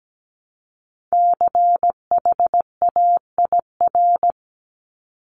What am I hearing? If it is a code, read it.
Morse code, heard as a keyed tone, about 17 wpm, 700 Hz. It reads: CHAIR
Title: Multi-letter Phonemes Repeated 3x 17wpm